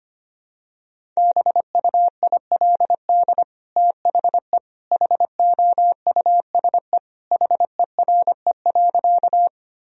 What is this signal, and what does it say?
Morse code, heard as a keyed tone, about 25 wpm, 700 Hz. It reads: BUILD THE HOUSE HERE.
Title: Sentences from Top 1000 Words - Review Part 1 - 25wpm